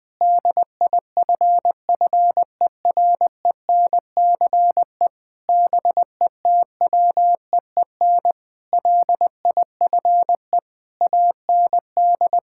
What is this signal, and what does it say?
Morse code, heard as a keyed tone, about 20 wpm, 700 Hz. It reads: DIFFERENCE BETWEEN LIFE AND
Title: Sentences with 8-Word Phrases - 20wpm